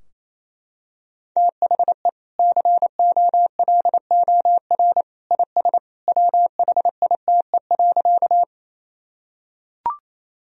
Morse code, heard as a keyed tone, about 28 wpm, 700 Hz. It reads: THE COLOR IS WHITE.
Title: Sentences from Top 1000 Words - Review Part 2 - 28wpm